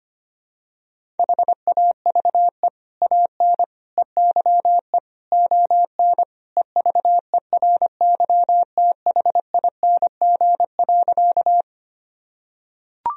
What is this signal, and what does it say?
Morse code, heard as a keyed tone, about 25 wpm, 700 Hz. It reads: HAVE AN EYE ON EVERYTHING.
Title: Sentences from Top 1000 Words - Review Part 1 - 25wpm